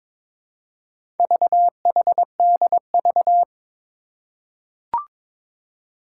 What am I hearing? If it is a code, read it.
Morse code, heard as a keyed tone, about 22 wpm, 700 Hz. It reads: VHDV
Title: Four Challenging Letter-Number Combinations 22wpm